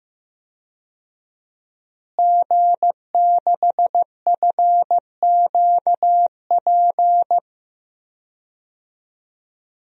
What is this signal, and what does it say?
Morse code, heard as a keyed tone, about 15 wpm, 700 Hz. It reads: G6FQP
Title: DXCC-100 Call Signs - 5 Characters - Repeated 3x - 15wpm